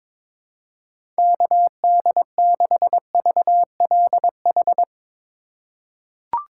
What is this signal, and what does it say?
Morse code, heard as a keyed tone, about 22 wpm, 700 Hz. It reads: KD6VLH
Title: USA Call Signs - 6 Characters - 3x Character Spacing - 22wpm